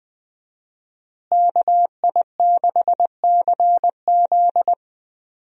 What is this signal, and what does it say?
Morse code, heard as a keyed tone, about 21 wpm, 700 Hz. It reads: KI6CZ
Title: USA Call Signs - 5 Characters - Repeated 2x - 20wpm